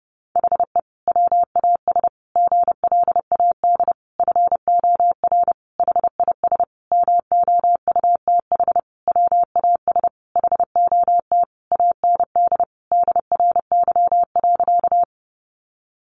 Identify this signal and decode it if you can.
Morse code, heard as a keyed tone, about 30 wpm, 700 Hz. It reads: HE WAS GLAD FOR HIS MOUTH WAS HOT AND DRY.
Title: Sentences from Top 1500 Words 30wpm